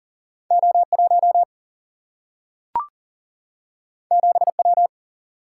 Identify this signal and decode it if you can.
Morse code, heard as a keyed tone, about 40 wpm, 700 Hz. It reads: O1 7W
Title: Single-Letter Single-Number 40wpm